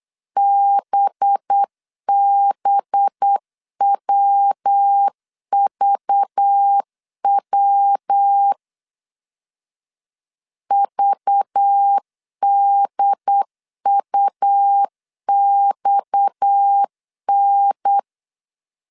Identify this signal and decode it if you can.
Morse code, heard as a keyed tone, about 8 wpm, 800 Hz. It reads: BBWVW VDUXN
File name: 8wpm